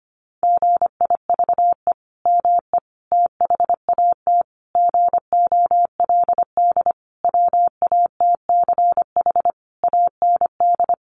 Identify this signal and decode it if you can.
Morse code, heard as a keyed tone, about 25 wpm, 700 Hz. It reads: GIVE ME THAT GOLD WATCH AND
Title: Sentences from Top 1500 Words 25wpm